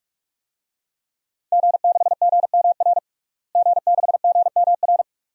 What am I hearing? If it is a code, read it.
Morse code, heard as a keyed tone, about 47 wpm, 700 Hz. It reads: G6GMR G6GMR
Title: DXCC-100 Call Signs - 5 Characters - Repeated 2x - 45wpm